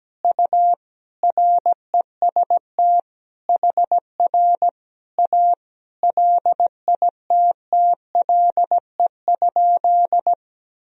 Morse code, heard as a keyed tone, about 17 wpm, 700 Hz. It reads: U REST HR A LITTLE?
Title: Sentences from Top 500 Words and Abbreviations 17wpm